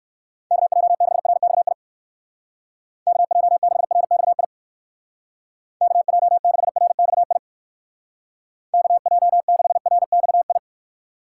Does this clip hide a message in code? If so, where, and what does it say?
Yes: Morse code from 0.5 to 10.6 s, heard as a keyed tone, about 62 wpm, 700 Hz. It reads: KJ6RXI KJ6RXI KJ6RXI KJ6RXI